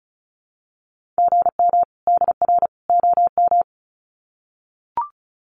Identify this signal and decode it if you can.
Morse code, heard as a keyed tone, about 34 wpm, 700 Hz. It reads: GM DR OM
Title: Common QSO Phrases 35wpm